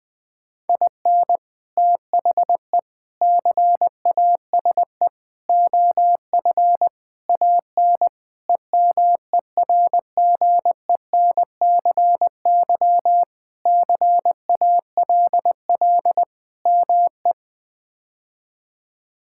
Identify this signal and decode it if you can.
Morse code, heard as a keyed tone, about 20 wpm, 700 Hz. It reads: IN THE CASE OF AN EMERGENCY CALL ME
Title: Sentences with 3-Word Phrases - 20wpm